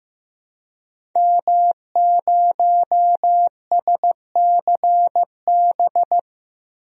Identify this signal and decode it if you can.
Morse code, heard as a keyed tone, about 15 wpm, 700 Hz. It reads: M0SCB
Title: DXCC-100 Call Signs - 5 Characters - 15wpm